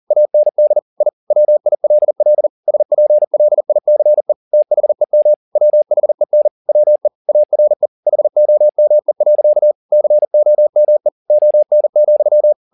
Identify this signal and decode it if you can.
Morse code, heard as a keyed tone, about 40 wpm, 600 Hz. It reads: AND I WILL SPLICE THEM WHEN WE ARE HOME. COME ON,
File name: the old man and the sea 2 40wpm